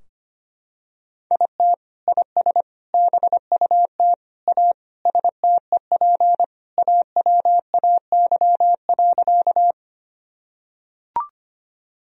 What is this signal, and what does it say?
Morse code, heard as a keyed tone, about 25 wpm, 700 Hz. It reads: IT IS BUT A STEP AWAY.
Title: Sentences from Top 1000 Words - Review Part 2 - 25wpm